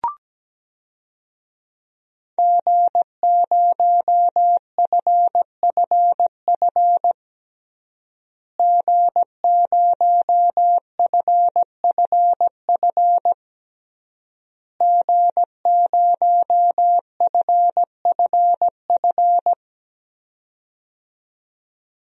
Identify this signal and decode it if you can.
Morse code, heard as a keyed tone, about 17 wpm, 700 Hz. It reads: G0FFF G0FFF G0FFF
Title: DXCC-100 Call Signs - 5 Characters - Repeated 3x - 17wpm